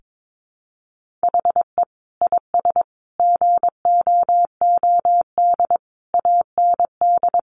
Morse code, heard as a keyed tone, about 22 wpm, 700 Hz. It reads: HE IS GOOD AND
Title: Sentences from Top 1400 Words 22wpm